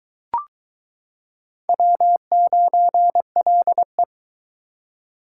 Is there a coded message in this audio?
Yes: Morse code from 1.7 to 4.0 s, heard as a keyed tone, about 22 wpm, 700 Hz. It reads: W9LE